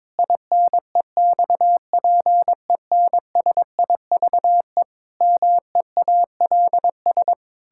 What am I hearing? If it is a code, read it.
Morse code, heard as a keyed tone, about 22 wpm, 700 Hz. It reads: INEXPENSIVE MEALS
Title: Pangram Challenge v2 - 22wpm